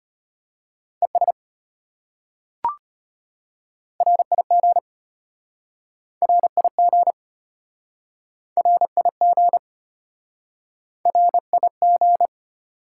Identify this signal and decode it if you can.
Morse code, heard as a keyed tone, about 33 wpm, 700 Hz. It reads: ES RIG RIG RIG RIG